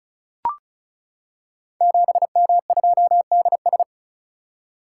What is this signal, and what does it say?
Morse code, heard as a keyed tone, about 36 wpm, 690 Hz. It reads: 7M2DS
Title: Code Groups - 5 Letter:Numbers - 35wpm